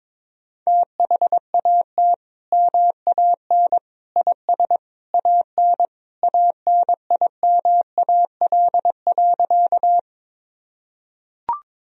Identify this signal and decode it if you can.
Morse code, heard as a keyed tone, about 22 wpm, 700 Hz. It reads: THAT MAN IS AN ANIMAL.
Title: Sentences from Top 1000 Words - Review Part 1 - 22wpm